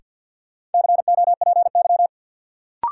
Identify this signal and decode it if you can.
Morse code, heard as a keyed tone, about 52 wpm, 700 Hz. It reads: COPY
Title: Sentences from Top 1400 Words - New Word Drill - 50wpm